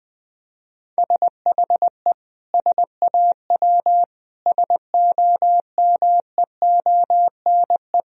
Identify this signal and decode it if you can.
Morse code, heard as a keyed tone, about 20 wpm, 700 Hz. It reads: SHE SAW SOMEONE
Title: Sentences from Top 1000 Words - Review Part 4 - 20wpm